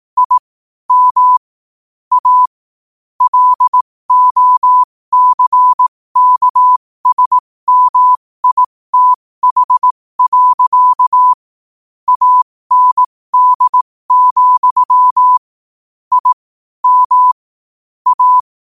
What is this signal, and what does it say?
Morse code, heard as a keyed tone, about 18 wpm, spaced as for 15 wpm, 1000 Hz. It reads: I M A LOCKSMITH. AND, I M A
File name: qotdfiles / 15wpm